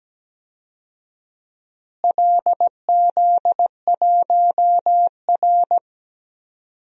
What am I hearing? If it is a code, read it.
Morse code, heard as a keyed tone, about 17 wpm, 700 Hz. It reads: LZ1R
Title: DXCC-100 Call Signs - 4 Characters - Repeated 2x - 17wpm